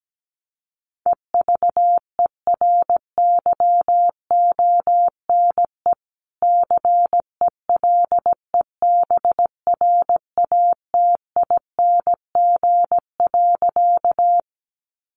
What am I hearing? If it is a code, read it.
Morse code, heard as a keyed tone, about 17 wpm, 700 Hz. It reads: EVERYONE CELEBRATING.
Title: A Sound Of Thunder 17wpm